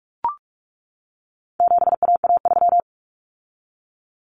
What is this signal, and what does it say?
Morse code, heard as a keyed tone, about 44 wpm, 700 Hz. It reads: 7AA3
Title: Common Cut Numbers - 4-Digit 45wpm